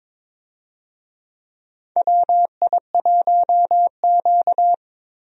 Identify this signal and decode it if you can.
Morse code, heard as a keyed tone, about 22 wpm, 700 Hz. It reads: WI1Q
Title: USA Call Signs - 4 Characters - 22wpm